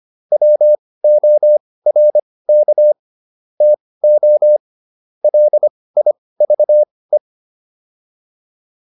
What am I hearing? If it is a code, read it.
Morse code, heard as a keyed tone, about 25 wpm, spaced as for 18 wpm, 600 Hz. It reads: WORK TO LIVE